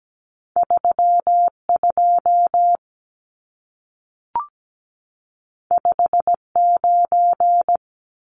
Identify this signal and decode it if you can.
Morse code, heard as a keyed tone, about 17 wpm, 700 Hz. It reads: 32 59
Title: Numbers - 2-Digit 17wpm